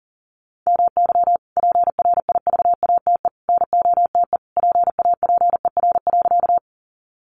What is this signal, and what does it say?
Morse code, heard as a keyed tone, about 37 wpm, 700 Hz. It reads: MY PRIVATE NOTE PAPER.
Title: Sentences from Top 1000 Words 40wpm